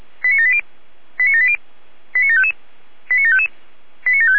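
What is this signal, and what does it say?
Five-tone selective call (ZVEI), heard as quick runs of five tones, tones of about 70 ms. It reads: 89780 89780 89755 89755 89756
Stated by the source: ZVEI five-tone selective call 89752, received by SDR.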